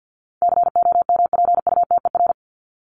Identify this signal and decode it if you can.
Morse code, heard as a keyed tone, about 50 wpm, 700 Hz. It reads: COMPUTER
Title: Top 501-750 Words 50wpm